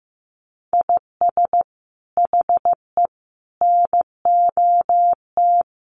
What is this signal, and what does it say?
Morse code, heard as a keyed tone, about 15 wpm, 700 Hz. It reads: IS HE NOT
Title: Sentences from Top 600 Words 15wpm